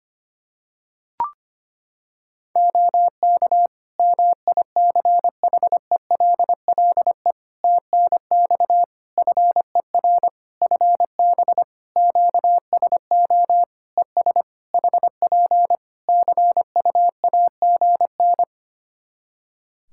Morse code, heard as a keyed tone, about 25 wpm, 700 Hz. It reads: OK MICHELLE TNX FER FB QSO ES HP CUAGN
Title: Practice QSO - Exchange 4 - Ending - 25wpm